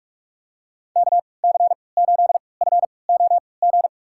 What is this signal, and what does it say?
Morse code, heard as a keyed tone, about 42 wpm, spaced as for 25 wpm, 700 Hz. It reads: KC8FOG
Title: USA Call Signs - 6 Characters - 3x Character Spacing - 45wpm